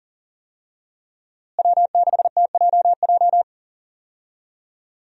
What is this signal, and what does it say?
Morse code, heard as a keyed tone, about 40 wpm, 700 Hz. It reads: W6TJJ
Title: USA Call Signs - 5 Characters - Repeated 3x - 40wpm